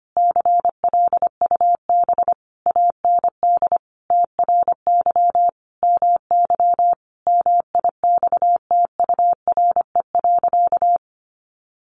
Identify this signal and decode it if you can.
Morse code, heard as a keyed tone, about 25 wpm, 700 Hz. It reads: CLUB AND TRY MY MIXTURE.
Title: Sentences from Top 1500 Words 25wpm